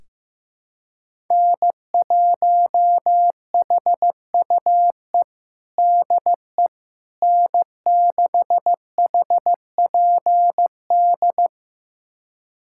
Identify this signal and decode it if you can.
Morse code, heard as a keyed tone, about 16 wpm, 700 Hz. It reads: N1HUE DE N6HPD
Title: USA Callsign DE Callsign 15wpm